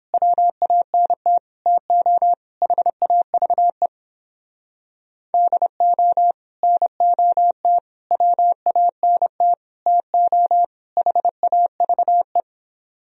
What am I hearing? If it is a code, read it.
Morse code, heard as a keyed tone, about 29 wpm, 700 Hz. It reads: WANT TO HAVE DO NOT WANT TO HAVE